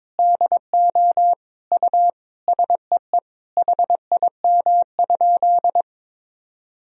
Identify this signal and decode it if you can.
Morse code, heard as a keyed tone, about 22 wpm, 700 Hz. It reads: DO U SEE HIM?
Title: Sentences from Top 100 Words and Abbreviations 22wpm